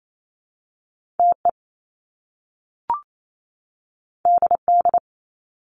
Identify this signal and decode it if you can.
Morse code, heard as a keyed tone, about 28 wpm, 700 Hz. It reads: TE DD
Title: Top Two Letter Combinations in Words 28wpm